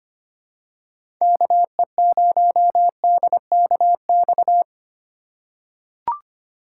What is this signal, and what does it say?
Morse code, heard as a keyed tone, about 25 wpm, 700 Hz. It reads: KE0DKX
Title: USA Call Signs - 6 Characters - 3x Character Spacing - 25wpm